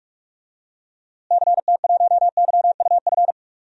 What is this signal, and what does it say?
Morse code, heard as a keyed tone, about 45 wpm, 700 Hz. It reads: CT1YUF
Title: DXCC-100 Call Signs - 6 Characters - Repeated 3x - 45wpm